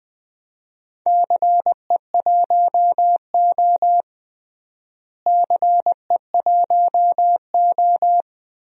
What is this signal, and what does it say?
Morse code, heard as a keyed tone, about 20 wpm, 700 Hz. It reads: CE1O CE1O
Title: DXCC-100 Call Signs - 4 Characters - Repeated 3x - 20wpm